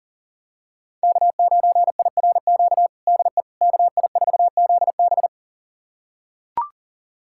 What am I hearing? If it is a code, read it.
Morse code, heard as a keyed tone, about 40 wpm, 700 Hz. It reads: K9IRQ DE KI4ZB
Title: USA Callsign DE Callsign - 5 Characters - 40wpm